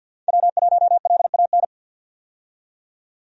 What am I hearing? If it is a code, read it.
Morse code, heard as a keyed tone, about 52 wpm, 700 Hz. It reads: W1LAN
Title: USA Call Signs - 5 Characters - 50wpm